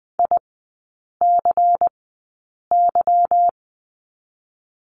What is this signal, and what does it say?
Morse code, heard as a keyed tone, about 20 wpm, spaced as for 8 wpm, 700 Hz. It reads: ICY
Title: 145 Lesson 24e - Letter y - Three Characters 20wpm